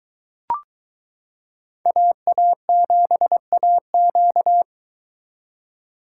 Morse code, heard as a keyed tone, about 22 wpm, 700 Hz. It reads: AA7AQ